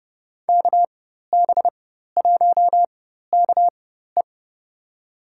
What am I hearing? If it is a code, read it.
Morse code, heard as a keyed tone, about 30 wpm, spaced as for 14 wpm, 700 Hz. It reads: KB1KE
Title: USA Call Signs - 5 Characters - 4x Character Spacing - 30wpm